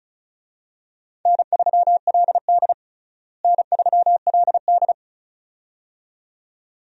Morse code, heard as a keyed tone, about 36 wpm, 700 Hz. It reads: N3LD N3LD
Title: USA Call Signs - 4 Characters - Repeated 2x - 35wpm